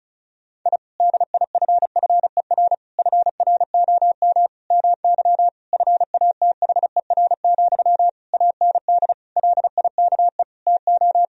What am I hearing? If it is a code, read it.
Morse code, heard as a keyed tone, about 36 wpm, 700 Hz. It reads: I DIFFER FROM MY FATHER, AND LIKE TO
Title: Sentences from Top 1000 Words - Review Part 1 - 35wpm